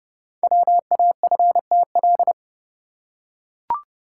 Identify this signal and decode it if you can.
Morse code, heard as a keyed tone, about 28 wpm, 690 Hz. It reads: WAFTL
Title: Code Groups - 5 Letters - 30wpm